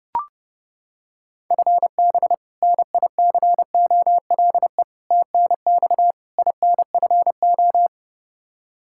Morse code, heard as a keyed tone, about 29 wpm, 700 Hz. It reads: FB NICOLE TNX INFO
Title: Practice QSO - Exchange 3 - More Info - 30wpm